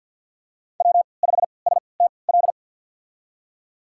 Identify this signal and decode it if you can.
Morse code, heard as a keyed tone, about 51 wpm, spaced as for 28 wpm, 700 Hz. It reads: W5STL
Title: USA Call Signs - 5 Characters - 3x Character Spacing - 50wpm